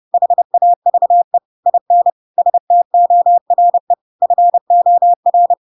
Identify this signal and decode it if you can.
Morse code, heard as a keyed tone, about 30 wpm, 700 Hz. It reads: HAVE IN STORE FOR
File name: Pinocchio / Pinocchio ch7 30 30 ik1hge